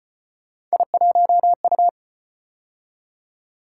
Morse code, heard as a keyed tone, about 33 wpm, 700 Hz. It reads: I1U